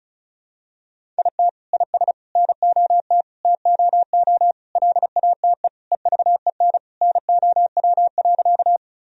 Morse code, heard as a keyed tone, about 36 wpm, 700 Hz. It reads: IT IS NOT TOO LATE EVEN NOW.
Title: Sentences from Top 1000 Words - Review Part 1 - 35wpm